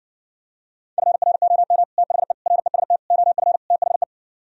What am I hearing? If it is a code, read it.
Morse code, heard as a keyed tone, about 70 wpm, 700 Hz. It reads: FROM THE REST OF THE